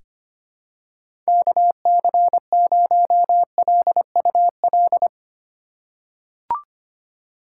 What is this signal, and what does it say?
Morse code, heard as a keyed tone, about 25 wpm, 700 Hz. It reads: KC0LUL
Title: USA Call Signs - 2x Character Spacing - 25wpm